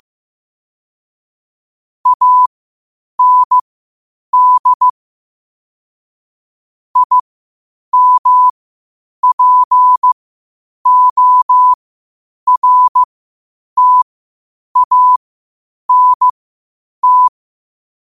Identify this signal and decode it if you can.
Morse code, heard as a keyed tone, about 15 wpm, spaced as for 8 wpm, 1000 Hz. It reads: AND IMPORTANT
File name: qotdfiles / 8wpm